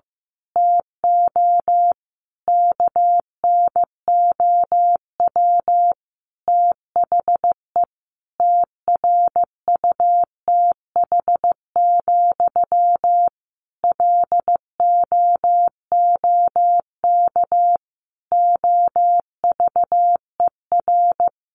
Morse code, heard as a keyed tone, about 15 wpm, 700 Hz. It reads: TO KNOW THE TRUTH, LOOK OVER
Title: Sentences from Top 800 Words 15wpm